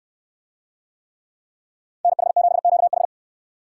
Morse code, heard as a keyed tone, about 64 wpm, 700 Hz. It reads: NH7QH